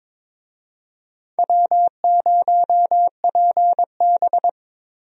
Morse code, heard as a keyed tone, about 23 wpm, 700 Hz. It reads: W0PB
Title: USA Call Signs - 4 Characters - 22wpm